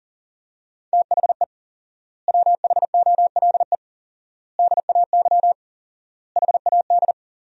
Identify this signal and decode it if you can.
Morse code, heard as a keyed tone, about 40 wpm, 700 Hz. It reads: THE WHOLE DAY HAD
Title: 4x Word-Spacing - Sentences from Top 500 Words - 40wpm